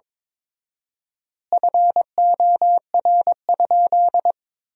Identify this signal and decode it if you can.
Morse code, heard as a keyed tone, about 22 wpm, 700 Hz. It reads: FOR?
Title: 4x Word-Spacing - Sentences from Top 100 Words - 22wpm